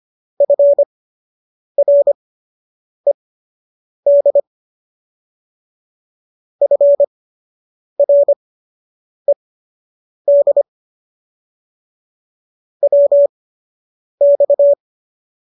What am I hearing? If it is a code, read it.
Morse code, heard as a keyed tone, about 25 wpm, spaced as for 8 wpm, 570 Hz. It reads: FRED FRED WX